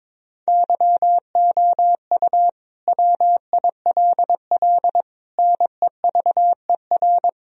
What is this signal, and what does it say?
Morse code, heard as a keyed tone, about 22 wpm, 700 Hz. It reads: YOU WILL NEVER
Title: Sentences from Top 1000 Words - Review Part 2 - 22wpm